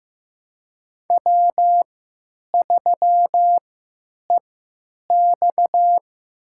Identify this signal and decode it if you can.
Morse code, heard as a keyed tone, about 15 wpm, spaced as for 9 wpm, 700 Hz. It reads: W3EX
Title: USA Call Signs - 3x Character Spacing - 15wpm